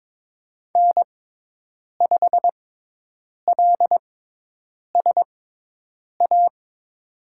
Morse code, heard as a keyed tone, about 22 wpm, spaced as for 8 wpm, 700 Hz. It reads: N5LSA
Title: USA Call Signs - 6x Character Spacing - 22wpm